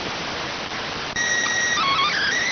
SSTV (slow-scan television), PD 290.